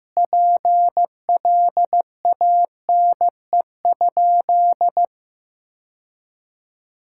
Morse code, heard as a keyed tone, about 15 wpm, 700 Hz. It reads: PLANE?
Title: Sentences from Top 1000 Words - Review Part 6 - 15wpm